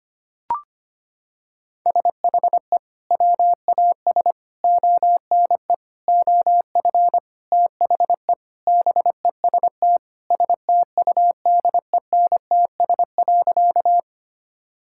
Morse code, heard as a keyed tone, about 25 wpm, 700 Hz. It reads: SHE WAS ONE OF THE BEST STUDENTS.
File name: practice / 25wpm / Sentences with 4-Word Phrases - 25wpm